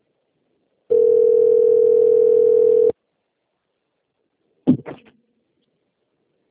Ringback tone (North America), 440 + 480 Hz.